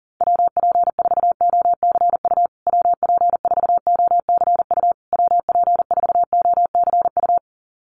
Morse code, heard as a keyed tone, about 40 wpm, 700 Hz. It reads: WP4OCU WP4OCU WP4OCU
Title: USA Call Signs Repeated 3x 40wpm